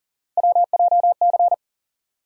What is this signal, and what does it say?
Morse code, heard as a keyed tone, about 40 wpm, 700 Hz. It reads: WJC
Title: USA Call Sign Postfixes - 3 Letters 40wpm